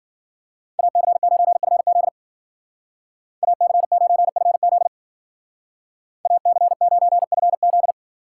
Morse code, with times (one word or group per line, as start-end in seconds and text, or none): 0.8-2.1 AC9FZ
3.4-4.9 AC9FZ
6.3-7.9 AC9FZ